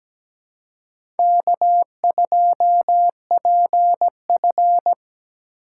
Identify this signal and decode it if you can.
Morse code, heard as a keyed tone, about 17 wpm, 700 Hz. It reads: K2PF
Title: USA Call Signs - 4x Character Spacing - 17wpm